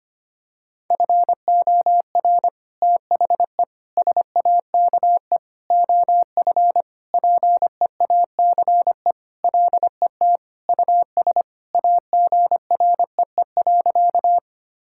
Morse code, heard as a keyed tone, about 25 wpm, 700 Hz. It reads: FOR THE SAKE OF PEACE LET US AGREE.
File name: practice / 25wpm / Sentences with 4-Word Phrases - 25wpm